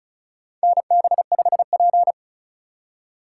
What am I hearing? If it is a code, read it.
Morse code, heard as a keyed tone, about 34 wpm, 700 Hz. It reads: NB5P
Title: USA Call Signs - 4 Characters - 4x Character Spacing - 35wpm